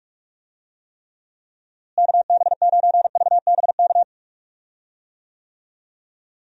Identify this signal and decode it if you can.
Morse code, heard as a keyed tone, about 42 wpm, 700 Hz. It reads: KB9VBK
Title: USA Call Signs - 6 Characters - Repeated 2x - 45wpm